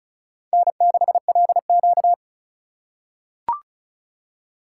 Morse code, heard as a keyed tone, about 36 wpm, 700 Hz. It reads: N6LQ
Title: USA Call Signs - 4 Characters - 2x Character Spacing - 35wpm